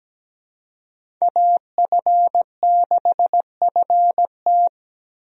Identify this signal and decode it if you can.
Morse code, heard as a keyed tone, about 17 wpm, 700 Hz. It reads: AF6FT
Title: USA Call Signs - 5 Characters - Repeated 3x - 17wpm